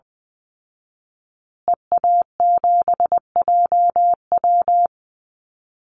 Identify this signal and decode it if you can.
Morse code, heard as a keyed tone, about 20 wpm, 700 Hz. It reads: EA7JW
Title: Call Signs Repeated 2x - 20wpm